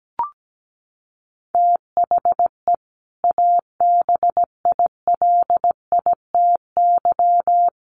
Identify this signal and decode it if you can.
Morse code, heard as a keyed tone, about 17 wpm, 700 Hz. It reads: THE ABILITY
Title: Sets of 3 Words 17wpm